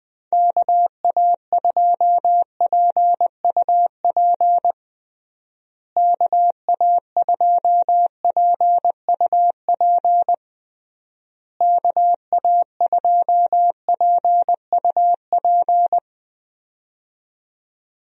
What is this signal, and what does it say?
Morse code, heard as a keyed tone, about 20 wpm, 700 Hz. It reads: KA2PUP KA2PUP KA2PUP
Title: USA Call Signs - 6 Characters - Repeated 3x - 20wpm